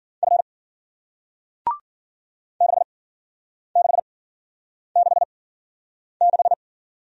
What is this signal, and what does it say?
Morse code, heard as a keyed tone, about 51 wpm, spaced as for 9 wpm, 700 Hz. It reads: F 6666